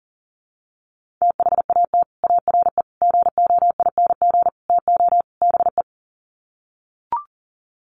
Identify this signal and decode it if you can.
Morse code, heard as a keyed tone, about 40 wpm, 700 Hz. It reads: THAT ARE GOING TO BE
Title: Sets of 5 Words Encore2 40wpm